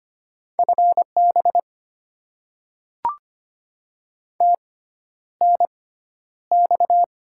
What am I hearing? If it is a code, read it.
Morse code, heard as a keyed tone, about 25 wpm, spaced as for 11 wpm, 700 Hz. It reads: FB TNX
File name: Top 20 QSO Elements - 6x Character Spacing - 25wpm